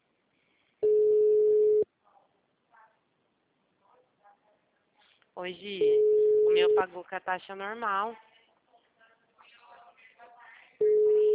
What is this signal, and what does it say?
Ringback tone (Europe, 425 Hz), with speech.
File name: out- -2022- - - .2275